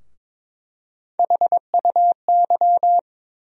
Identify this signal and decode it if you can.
Morse code, heard as a keyed tone, about 23 wpm, 700 Hz. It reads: HUY
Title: USA Call Sign Postfixes - 3 Letters 22wpm